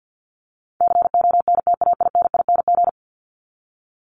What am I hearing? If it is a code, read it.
Morse code, heard as a keyed tone, about 50 wpm, 700 Hz. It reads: CONTAINING
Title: ING Suffix - Top 100 Words with Base Words 50wpm